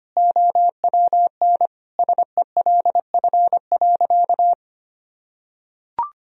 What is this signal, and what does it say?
Morse code, heard as a keyed tone, about 25 wpm, 700 Hz. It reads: OWN SELF.
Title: Sentences from Top 1000 Words - Review Part 1 - 25wpm